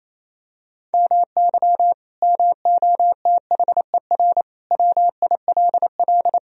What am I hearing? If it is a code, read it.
Morse code, heard as a keyed tone, about 28 wpm, 700 Hz. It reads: MY MOTHER WILL
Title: Sentences from Top 1000 Words - Review Part 1 - 28wpm